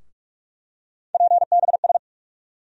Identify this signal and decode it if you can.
Morse code, heard as a keyed tone, about 46 wpm, 700 Hz. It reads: PBS
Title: USA Call Sign Postfixes - 3 Letters 45wpm